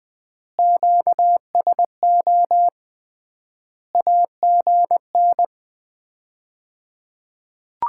Morse code, heard as a keyed tone, about 20 wpm, 700 Hz. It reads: QSO AGN
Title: Common QSO Phrases - 3x Word-Spacing - 20wpm